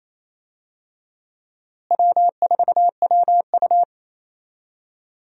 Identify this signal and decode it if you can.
Morse code, heard as a keyed tone, about 28 wpm, 700 Hz. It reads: W4WU
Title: USA Call Signs - 4 Characters - Repeated 2x - 28wpm